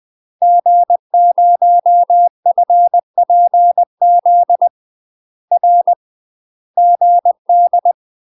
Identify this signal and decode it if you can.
Morse code, heard as a keyed tone, about 20 wpm, 700 Hz. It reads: G0FPZ R GD
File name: UK MASTER QSOs TRAINING / 20WPM / UK MASTER QSOs 0015